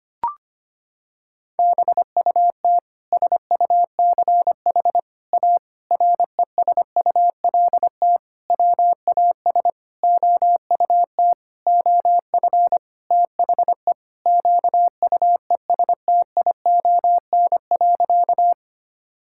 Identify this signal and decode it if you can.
Morse code, heard as a keyed tone, about 25 wpm, 700 Hz. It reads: BUT SUCH A RESULT WAS OUT OF THE QUESTION.
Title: Sentences from Top 1000 Words - Review Part 5 - 25wpm